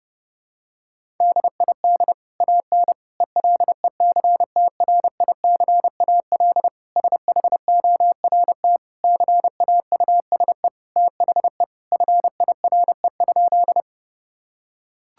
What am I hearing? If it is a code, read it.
Morse code, heard as a keyed tone, about 30 wpm, 700 Hz. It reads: DID AN ELECTRICAL SHORT CAUSE THE FIRE?
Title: Sentences from Top 1000 Words - Review Part 6 - 30wpm